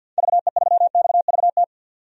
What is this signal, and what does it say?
Morse code, heard as a keyed tone, about 52 wpm, 700 Hz. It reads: VE3XVT
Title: DXCC-100 Call Signs - 6 Characters - 50wpm